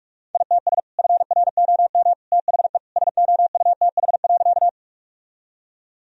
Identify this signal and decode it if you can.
Morse code, heard as a keyed tone, about 47 wpm, 700 Hz. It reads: ITS FROM THE SOUTH.
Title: Sentences from Top 1000 Words - Review Part 2 - 45wpm